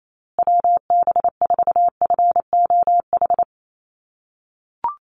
Morse code, heard as a keyed tone, about 28 wpm, 700 Hz. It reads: WB4FOH
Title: USA Call Signs Repeated 3x 28wpm